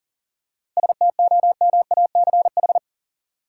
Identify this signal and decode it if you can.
Morse code, heard as a keyed tone, about 40 wpm, 700 Hz. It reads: STOMACH
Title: Seven Letter Words - 40wpm